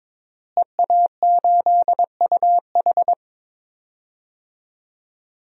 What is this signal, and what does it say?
Morse code, heard as a keyed tone, about 22 wpm, 700 Hz. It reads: EA8UH
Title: DXCC-100 Call Signs - 5 Characters - Repeated 2x - 22wpm